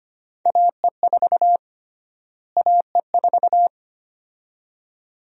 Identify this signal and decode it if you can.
Morse code, heard as a keyed tone, about 25 wpm, 700 Hz. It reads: AE4 AE4
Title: USA Call Sign Prefixes - Repeated 2x - 25wpm